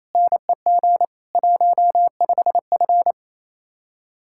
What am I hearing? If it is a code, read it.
Morse code, heard as a keyed tone, about 35 wpm, 700 Hz. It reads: NEG 15F
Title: Speed-Racing Common QSO Phrases - 28wpm